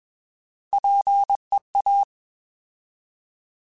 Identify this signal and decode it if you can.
Morse code, heard as a keyed tone, about 21 wpm, 800 Hz. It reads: PEA
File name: Alphabetical / Alphabetical mp3s / Ps @20